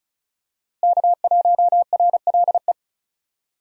Morse code, heard as a keyed tone, about 36 wpm, 700 Hz. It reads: K1RLE
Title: USA Call Signs - 5 Characters - Repeated 3x - 35wpm